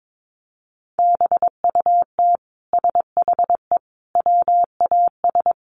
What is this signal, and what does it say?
Morse code, heard as a keyed tone, about 22 wpm, 700 Hz. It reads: BUT SHE WAS
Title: Sentences from Top 700 Words 22wpm